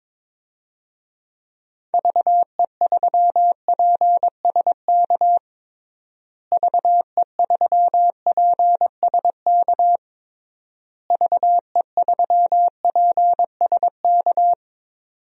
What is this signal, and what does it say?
Morse code, heard as a keyed tone, about 22 wpm, 700 Hz. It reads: VE3PSK VE3PSK VE3PSK
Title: DXCC-100 Call Signs - 6 Characters - Repeated 3x - 22wpm